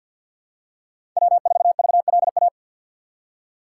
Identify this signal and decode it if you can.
Morse code, heard as a keyed tone, about 52 wpm, 700 Hz. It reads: W4VLA
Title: USA Call Signs - 5 Characters - 4x Character Spacing - 50wpm